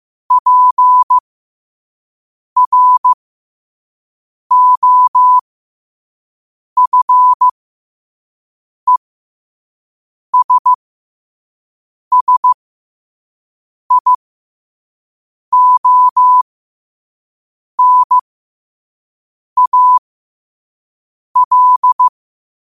Morse code, heard as a keyed tone, about 15 wpm, spaced as for 5 wpm, 1000 Hz. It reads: PROFESSIONAL
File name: qotdfiles / 5wpm